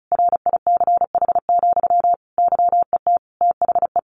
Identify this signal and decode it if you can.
Morse code, heard as a keyed tone, about 34 wpm, 700 Hz. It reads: RICH, YET THE
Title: Sentences from Top 1100 Words 35wpm